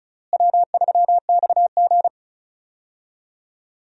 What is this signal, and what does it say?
Morse code, heard as a keyed tone, about 36 wpm, 700 Hz. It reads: W3XG
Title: USA Call Signs - 4 Characters - Repeated 3x - 35wpm